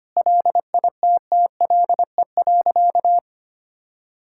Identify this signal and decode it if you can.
Morse code, heard as a keyed tone, about 25 wpm, 700 Hz. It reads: LITTLE.
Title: Sentences from Top 1600 Words 25wpm